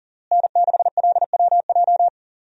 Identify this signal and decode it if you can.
Morse code, heard as a keyed tone, about 37 wpm, 700 Hz. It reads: N6LWJ
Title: USA Call Signs - 5 Characters - Repeated 3x - 40wpm